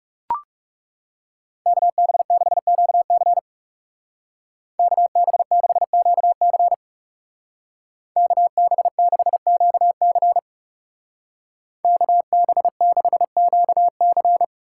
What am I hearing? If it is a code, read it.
Morse code, heard as a keyed tone, about 36 wpm, 700 Hz. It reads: KB6QC KB6QC KB6QC KB6QC
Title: Speed Racing - USA 2x2 Call Signs 30wpm